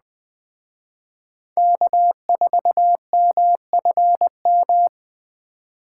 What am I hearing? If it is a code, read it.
Morse code, heard as a keyed tone, about 20 wpm, 700 Hz. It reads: K4MFM
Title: Speed Racing - USA 1x3 Call Signs 17wpm